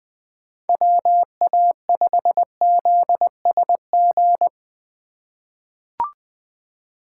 Morse code, heard as a keyed tone, about 20 wpm, 700 Hz. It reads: WA5ZSG
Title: USA Call Signs - 6 Characters - 2x Character Spacing - 20wpm